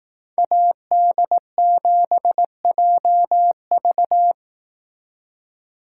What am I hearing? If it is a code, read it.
Morse code, heard as a keyed tone, about 18 wpm, 700 Hz. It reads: AD7JV
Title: Speed Racing - USA 2x2 Call Signs 15wpm